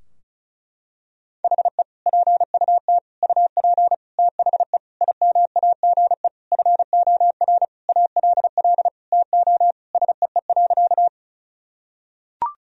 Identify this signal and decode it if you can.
Morse code, heard as a keyed tone, about 36 wpm, 700 Hz. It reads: HE PUT UP THE IMAGE FOR ALL TO SEE.
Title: Sentences from Top 1000 Words - Review Part 4 - 35wpm